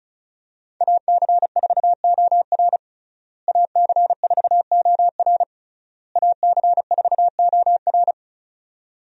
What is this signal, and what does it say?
Morse code, heard as a keyed tone, about 36 wpm, 700 Hz. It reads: AC4OR AC4OR AC4OR
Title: USA Call Signs - 5 Characters - Repeated 3x - 35wpm